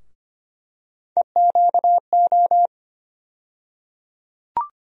Morse code, heard as a keyed tone, about 25 wpm, 700 Hz. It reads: EQO
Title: USA Call Sign Postfixes - 3 Letters 25wpm